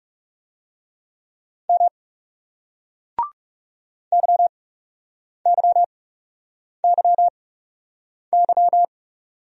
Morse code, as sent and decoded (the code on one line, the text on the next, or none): -- / -.-- -.-- -.-- -.--
M YYYY